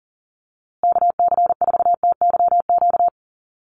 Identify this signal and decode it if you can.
Morse code, heard as a keyed tone, about 40 wpm, 700 Hz. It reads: KC4TYQ
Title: USA Call Signs Repeated 3x 40wpm